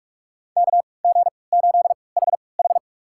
Morse code, heard as a keyed tone, about 43 wpm, spaced as for 25 wpm, 700 Hz. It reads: KG8HH
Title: USA Call Signs - 5 Characters - 3x Character Spacing - 45wpm